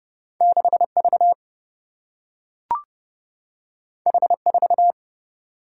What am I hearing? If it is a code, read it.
Morse code, heard as a keyed tone, about 30 wpm, 700 Hz. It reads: 6 V H 4
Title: Single-Letter Single-Number 30wpm